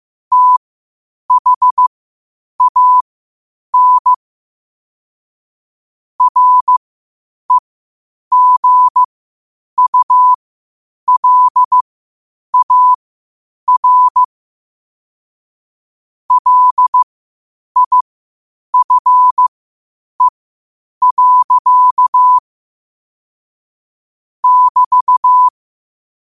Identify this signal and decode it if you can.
Morse code, heard as a keyed tone, about 15 wpm, spaced as for 8 wpm, 1000 Hz. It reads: THAN REGULAR LIFE. <BT>
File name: qotdfiles / 8wpm